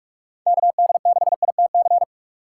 Morse code, heard as a keyed tone, about 44 wpm, 700 Hz. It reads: KD6ITC